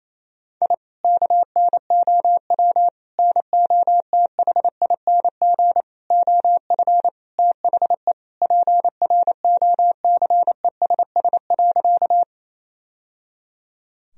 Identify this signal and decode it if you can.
Morse code, heard as a keyed tone, about 28 wpm, 700 Hz. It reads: I KNOW NOTHING OF THE PROCESS.
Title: Sentences from Top 1000 Words - Review Part 3 - 28wpm